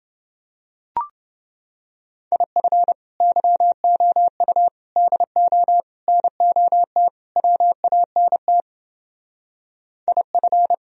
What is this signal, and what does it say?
Morse code, heard as a keyed tone, about 29 wpm, 700 Hz. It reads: IF YOU DO NOT WANT IF